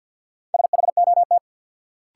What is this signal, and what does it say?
Morse code, heard as a keyed tone, about 46 wpm, 700 Hz. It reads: SHOT
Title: Sentences from Top 1100 Words - New Word Drill Alt - 50wpm